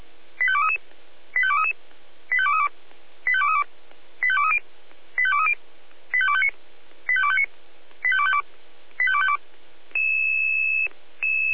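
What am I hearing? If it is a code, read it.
Five-tone selective call (ZVEI), heard as quick runs of five tones, tones of about 70 ms. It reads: 85233 85233 85232 85232 85230 85230 85259 85259 85252 85252
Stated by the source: ZVEI five-tone call received by SDR.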